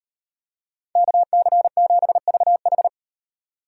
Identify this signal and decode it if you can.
Morse code, heard as a keyed tone, about 38 wpm, 700 Hz. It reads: KC7VH